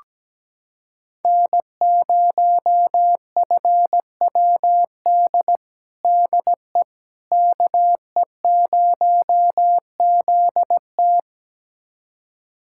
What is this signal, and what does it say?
Morse code, heard as a keyed tone, about 17 wpm, 700 Hz. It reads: N0FWD DE KE0ZT
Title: USA Callsign DE Callsign - 5 Characters - 17wpm